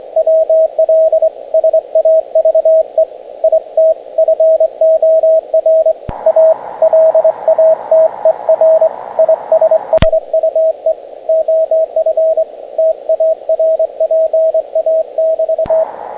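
Morse code, heard as a keyed tone, about 23 wpm, 640 Hz, with radio noise. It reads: WL SAVE IT FOR A LATER ISSUE OF TARPA<BT>
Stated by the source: Morse code demonstrating the SCAF-1 bandpass noise filter.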